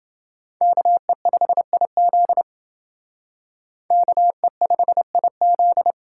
Morse code, heard as a keyed tone, about 29 wpm, 700 Hz. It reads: KE5IZ KE5IZ